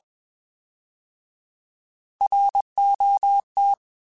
Morse code, heard as a keyed tone, about 21 wpm, 800 Hz. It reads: ROT